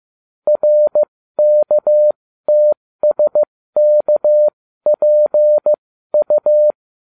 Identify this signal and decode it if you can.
Morse code, heard as a keyed tone, about 15 wpm, spaced as for 13 wpm, 600 Hz. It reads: RKTSKPU